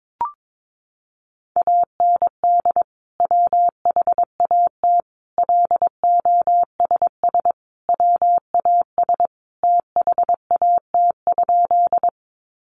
Morse code, heard as a keyed tone, about 22 wpm, 700 Hz. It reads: AND WHAT LOSS WAS THAT?
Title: Sentences from Top 1400 Words 22wpm